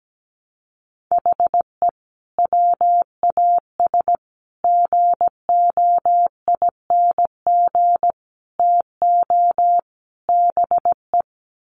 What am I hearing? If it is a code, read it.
Morse code, heard as a keyed tone, about 17 wpm, 700 Hz. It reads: HE WAS GOING TO BE
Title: Sets of 5 Words 17wpm